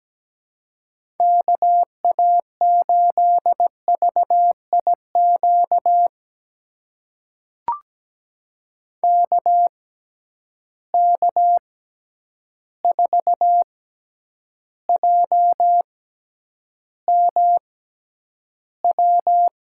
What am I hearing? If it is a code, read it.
Morse code, heard as a keyed tone, about 17 wpm, spaced as for 9 wpm, 700 Hz. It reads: KA8VIQ KK4JMW
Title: USA Call Signs - 6 Characters - 6x Character Spacing - 17wpm